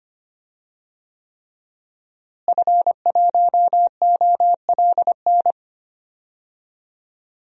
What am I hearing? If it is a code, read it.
Morse code, heard as a keyed tone, about 25 wpm, 700 Hz. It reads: F1OLN
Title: DXCC-100 Call Signs - 5 Characters - Repeated 3x - 25wpm